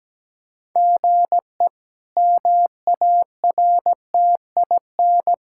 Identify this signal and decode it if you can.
Morse code, heard as a keyed tone, about 17 wpm, 700 Hz. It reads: GE MARTIN
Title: K1USN SST QSO Practice - 17wpm